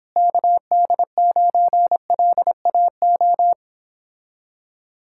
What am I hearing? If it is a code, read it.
Morse code, heard as a keyed tone, about 29 wpm, 700 Hz. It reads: KD9LAO